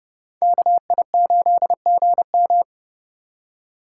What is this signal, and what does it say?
Morse code, heard as a keyed tone, about 28 wpm, 700 Hz. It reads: KI8GM